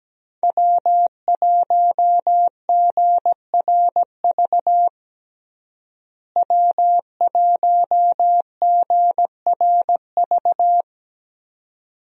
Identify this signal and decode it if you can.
Morse code, heard as a keyed tone, about 17 wpm, 700 Hz. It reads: W1GRV W1GRV
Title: USA Call Signs - 5 Characters - Repeated 2x - 17wpm